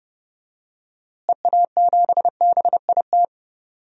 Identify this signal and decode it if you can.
Morse code, heard as a keyed tone, about 30 wpm, 700 Hz. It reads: EA7BIT
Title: DXCC-100 Call Signs - 6 Characters - Repeated 2x - 30wpm